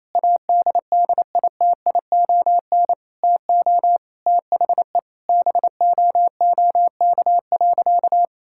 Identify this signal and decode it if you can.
Morse code, heard as a keyed tone, about 28 wpm, 700 Hz. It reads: ADDITION TO THE BOOK.
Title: Sentences from Top 1000 Words - Review Part 6 - 28wpm